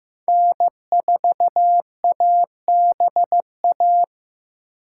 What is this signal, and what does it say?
Morse code, heard as a keyed tone, about 15 wpm, 700 Hz. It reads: N4ABA
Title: USA Call Signs - 5 Characters - 4x Character Spacing - 15wpm